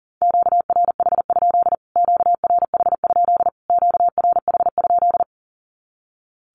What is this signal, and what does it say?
Morse code, heard as a keyed tone, about 40 wpm, 700 Hz. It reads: QRH? QRH? QRH?
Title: Q-Codes Repeated 3x 40wpm